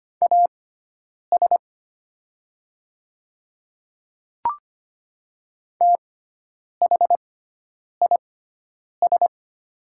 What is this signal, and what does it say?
Morse code, heard as a keyed tone, about 25 wpm, spaced as for 9 wpm, 690 Hz. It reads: AS THIS
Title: 6x Character Spacing - Top 100 Words - 25wpm